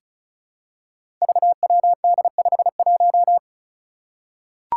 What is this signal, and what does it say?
Morse code, heard as a keyed tone, about 36 wpm, 700 Hz. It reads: VWD51
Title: Five Challenging Letter-Number Combinations 35wpm